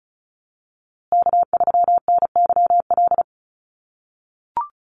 Morse code, heard as a keyed tone, about 34 wpm, 700 Hz. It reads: K3NYL
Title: USA Call Signs Repeated 2x 35wpm